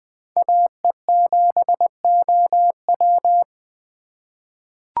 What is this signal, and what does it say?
Morse code, heard as a keyed tone, about 21 wpm, 700 Hz. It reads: AE7OW
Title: USA Call Signs - 3x Character Spacing - 20wpm